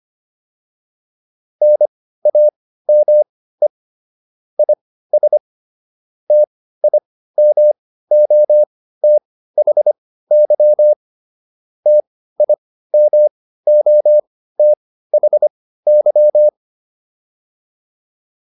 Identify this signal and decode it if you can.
Morse code, heard as a keyed tone, about 25 wpm, spaced as for 15 wpm, 600 Hz. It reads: NAME IS TIMOTHY TIMOTHY